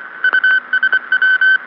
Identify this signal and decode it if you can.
Morse code, heard as a keyed tone, about 24 wpm, 1550 Hz. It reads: USW